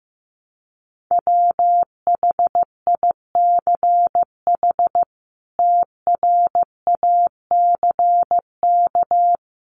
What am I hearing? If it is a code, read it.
Morse code, heard as a keyed tone, about 15 wpm, 700 Hz. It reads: WHICH TRACK
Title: Sentences from Top 1100 Words 15wpm